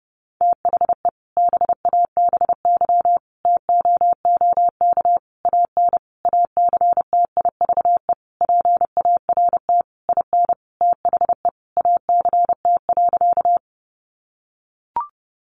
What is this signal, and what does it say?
Morse code, heard as a keyed tone, about 30 wpm, 700 Hz. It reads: THE BABY TOOK AN ACTIVE PART IN THE ACT.
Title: Sentences from Top 1300 Words 30wpm